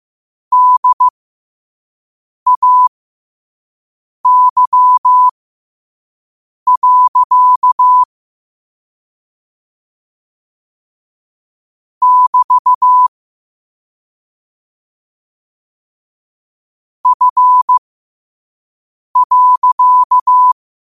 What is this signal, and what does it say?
Morse code, heard as a keyed tone, about 15 wpm, spaced as for 5 wpm, 1000 Hz. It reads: DAY. <BT> F.
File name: qotdfiles / 5wpm